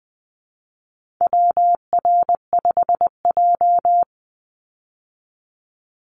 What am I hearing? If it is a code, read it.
Morse code, heard as a keyed tone, about 20 wpm, 700 Hz. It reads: WR5J
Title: Call Signs Repeated 2x - 20wpm